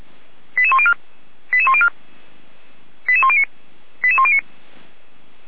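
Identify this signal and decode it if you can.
Five-tone selective call (ZVEI), heard as quick runs of five tones, tones of about 80 ms. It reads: 88104 88104 88109 88109
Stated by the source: ZVEI five-tone selective call 88523, received by SDR.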